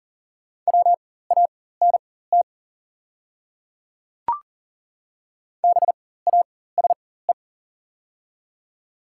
Morse code, heard as a keyed tone, about 40 wpm, spaced as for 19 wpm, 700 Hz. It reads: WANT BASE
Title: 4x Character Spacing - Top 300 Words - 40wpm